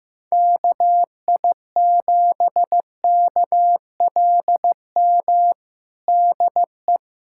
Morse code, heard as a keyed tone, about 14 wpm, 700 Hz. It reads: KI7KLM DE
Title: USA Callsign DE Callsign - 6 Characters - 15wpm